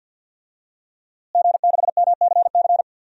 Morse code, heard as a keyed tone, about 52 wpm, 700 Hz. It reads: G6MCC
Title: DXCC-100 Call Signs - 5 Characters - Repeated 2x - 50wpm